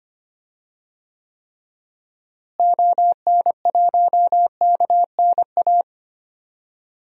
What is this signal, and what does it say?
Morse code, heard as a keyed tone, about 25 wpm, 700 Hz. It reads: ON1KNA
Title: DXCC-100 Call Signs - 6 Characters - Repeated 3x - 25wpm